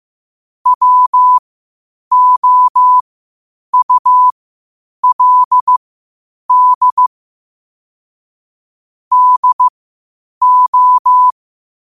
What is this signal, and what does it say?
Morse code, heard as a keyed tone, about 15 wpm, spaced as for 8 wpm, 1000 Hz. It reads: WOULD DO
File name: qotdfiles / 8wpm